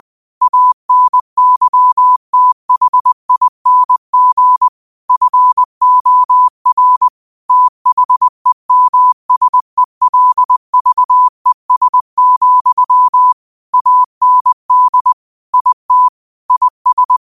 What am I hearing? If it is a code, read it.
Morse code, heard as a keyed tone, about 20 wpm, 1000 Hz. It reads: ANYTHING FOR THEMSELVES, AND IT IS
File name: qotdfiles / 20wpm